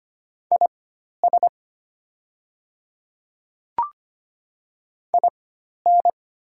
Morse code, heard as a keyed tone, about 25 wpm, spaced as for 12 wpm, 690 Hz. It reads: IS IN
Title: 4x Character Spacing - Top 100 Words - 25wpm